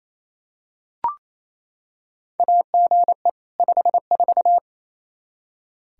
Morse code, heard as a keyed tone, about 28 wpm, 700 Hz. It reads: AGE 54
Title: Practice QSO - Exchange 3 - More Info - 28wpm